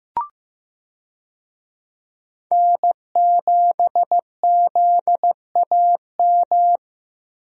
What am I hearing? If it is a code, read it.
Morse code, heard as a keyed tone, about 15 wpm, 700 Hz. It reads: N7ZAM
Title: USA Call Signs - 5 Characters - Repeated 3x - 15wpm